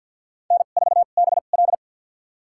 Morse code, heard as a keyed tone, about 51 wpm, spaced as for 36 wpm, 700 Hz. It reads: N4BL
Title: USA Call Signs - 4 Characters - 2x Character Spacing - 50wpm